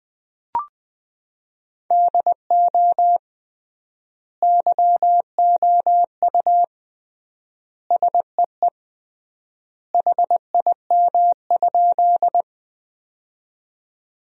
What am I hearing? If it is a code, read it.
Morse code, heard as a keyed tone, about 20 wpm, 700 Hz. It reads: DO YOU SEE HIM?
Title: 3x Word-Spacing - Sentences from Top 100 Words - 20wpm